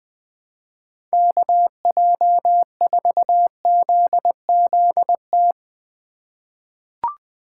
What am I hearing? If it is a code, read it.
Morse code, heard as a keyed tone, about 20 wpm, 700 Hz. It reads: KJ4ZZT
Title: USA Call Signs - 6 Characters - 2x Character Spacing - 20wpm